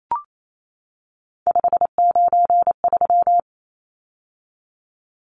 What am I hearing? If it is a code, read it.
Morse code, heard as a keyed tone, about 28 wpm, 700 Hz. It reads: 593
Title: Numbers - 3-Digit 28wpm